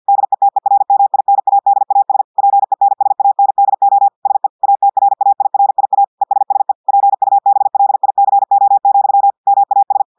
Morse code, heard as a keyed tone, about 50 wpm, 800 Hz. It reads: DETERMINADAS PENSANDO SE ATRAIRIA ESSE PUBLICO, MAS
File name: portugais texte en clair 2 vitesse 50 mots minutes